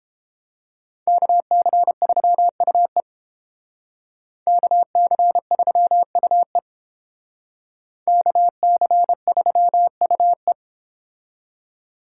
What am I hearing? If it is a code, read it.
Morse code, heard as a keyed tone, about 30 wpm, 700 Hz. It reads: KC3UE KC3UE KC3UE